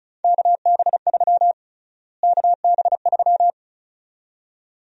Morse code, heard as a keyed tone, about 36 wpm, 700 Hz. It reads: KB3 KB3
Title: USA Call Sign Prefixes - Repeated 2x - 35wpm